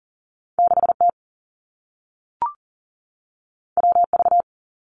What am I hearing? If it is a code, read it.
Morse code, heard as a keyed tone, about 38 wpm, 700 Hz. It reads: 6 T W V
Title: Two Challenging Letter-Number Combinations 40wpm